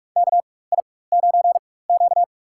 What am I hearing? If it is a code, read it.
Morse code, heard as a keyed tone, about 42 wpm, spaced as for 20 wpm, 700 Hz. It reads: KI9Q
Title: USA Call Signs - 4 Characters - 4x Character Spacing - 45wpm